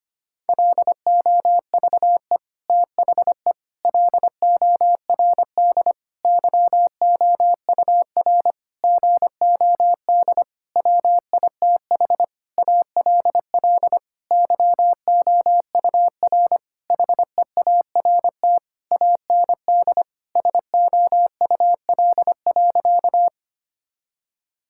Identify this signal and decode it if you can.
Morse code, heard as a keyed tone, about 25 wpm, 700 Hz. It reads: LOVE THE LORD YOUR GOD WITH ALL YOUR HEART AND SOUL.
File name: practice / 25wpm / Sentences with 9-Word Phrases - 25wpm